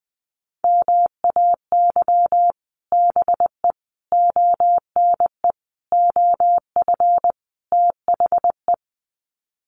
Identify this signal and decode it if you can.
Morse code, heard as a keyed tone, about 20 wpm, 700 Hz. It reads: MAY BE ONE OF THE
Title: Sets of 5 Words Encore2 20wpm